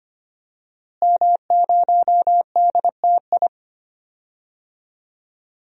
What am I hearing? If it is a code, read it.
Morse code, heard as a keyed tone, about 25 wpm, 700 Hz. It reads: M0DTI
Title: DXCC-100 Call Signs - 5 Characters - Repeated 3x - 25wpm